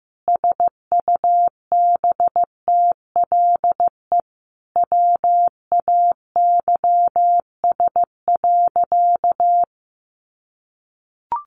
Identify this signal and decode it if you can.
Morse code, heard as a keyed tone, about 15 wpm, 700 Hz. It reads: SUBTLE WAYS.
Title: A Sound Of Thunder 15wpm